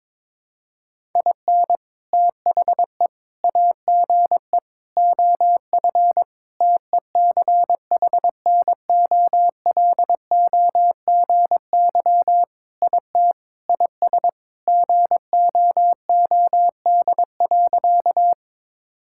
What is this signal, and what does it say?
Morse code, heard as a keyed tone, about 22 wpm, 700 Hz. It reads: IN THE AGE OF TECHNOLOGY IT IS GOOD.
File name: practice / 22wpm / Sentences with 3-Word Phrases - 22wpm